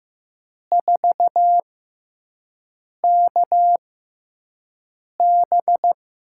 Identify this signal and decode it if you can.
Morse code, heard as a keyed tone, about 15 wpm, spaced as for 5 wpm, 700 Hz. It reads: 4KB
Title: USA Call Signs - 6 Characters - 6x Character Spacing - 15wpm